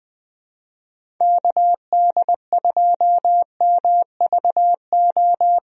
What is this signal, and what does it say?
Morse code, heard as a keyed tone, about 20 wpm, 700 Hz. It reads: KD2MVO
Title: USA Call Signs - 6 Characters - Repeated 2x - 20wpm